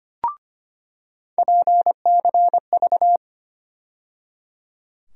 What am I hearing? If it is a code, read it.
Morse code, heard as a keyed tone, about 25 wpm, 700 Hz. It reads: PCV
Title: USA Call Sign Postfixes - 3 Letters 25wpm